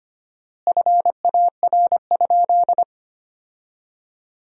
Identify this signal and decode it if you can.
Morse code, heard as a keyed tone, about 25 wpm, 700 Hz. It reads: FAR?
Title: 4x Word-Spacing - Sentences from Top 300 Words - 25wpm